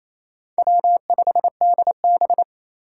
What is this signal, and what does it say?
Morse code, heard as a keyed tone, about 28 wpm, 700 Hz. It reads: W5DB
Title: USA Call Signs - 4 Characters - 2x Character Spacing - 28wpm